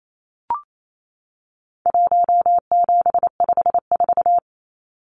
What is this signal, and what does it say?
Morse code, heard as a keyed tone, about 28 wpm, 700 Hz. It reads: 1754
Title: Numbers - Contest Style - 4-Digit 28wpm